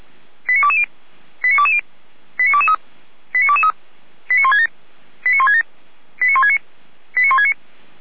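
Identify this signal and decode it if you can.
Five-tone selective call (ZVEI), heard as quick runs of five tones, tones of about 70 ms. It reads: 89220 89220 89223 89223 89167 89167 89169 89169
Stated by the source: ZVEI five-tone call 89123 received by SDR, logged by feuerwehr-orlamuende.org fire brigade server.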